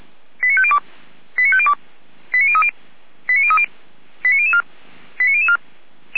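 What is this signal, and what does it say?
Five-tone selective call (ZVEI), heard as quick runs of five tones, tones of about 80 ms. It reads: 89502 89502 89030 89030 89004 89004
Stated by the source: ZVEI five-tone call 89001 received by SDR, channel 455, from feuerwehr-orlamuende.org fire brigade server.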